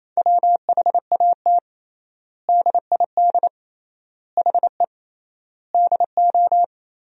Morse code, heard as a keyed tone, about 28 wpm, 700 Hz. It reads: WHAT DID HE DO
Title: 3x Word-Spacing - Sentences from Top 100 Words - 28wpm